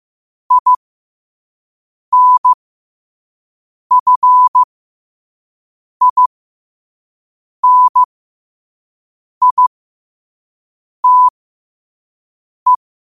Morse code, heard as a keyed tone, about 15 wpm, spaced as for 5 wpm, 1000 Hz. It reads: INFINITE
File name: qotdfiles / 5wpm